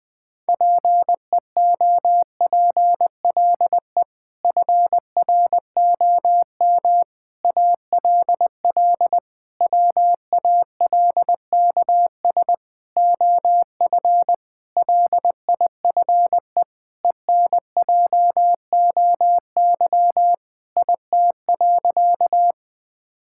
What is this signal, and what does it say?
Morse code, heard as a keyed tone, about 20 wpm, 700 Hz. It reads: PEOPLE FROM ALL WALKS OF LIFE ENJOY IT.
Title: Sentences with 6-Word Phrases - 20wpm